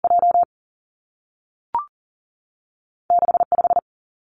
Morse code, heard as a keyed tone, about 42 wpm, 700 Hz. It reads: J 6 5
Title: Two Challenging Letter-Number Combinations 40wpm